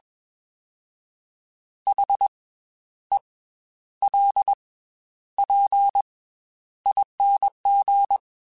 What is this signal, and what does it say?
Morse code, heard as a keyed tone, about 21 wpm, 800 Hz. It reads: H E L P ING
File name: Special INGs / Special INGs-mp3s / Slow ing @20